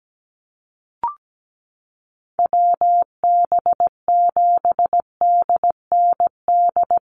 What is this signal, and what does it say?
Morse code, heard as a keyed tone, about 17 wpm, 700 Hz. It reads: WB7DND
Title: Call Signs Repeated 2x - 17wpm